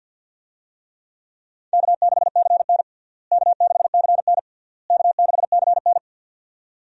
Morse code, heard as a keyed tone, about 51 wpm, 700 Hz. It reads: K6CN K6CN K6CN
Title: USA Call Signs - 4 Characters - Repeated 3x - 50wpm